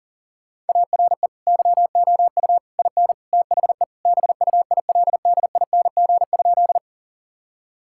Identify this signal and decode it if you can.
Morse code, heard as a keyed tone, about 40 wpm, 700 Hz. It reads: ARE YOU IN THE BUILDING?
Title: Sentences from Top 1000 Words - Review Part 5 - 40wpm